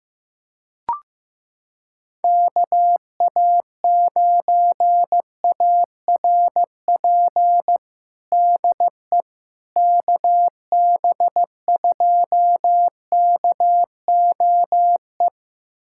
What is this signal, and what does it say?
Morse code, heard as a keyed tone, about 15 wpm, 700 Hz. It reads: KA9ARP DE KB2KOE
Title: USA Callsign DE Callsign - 6 Characters - 15wpm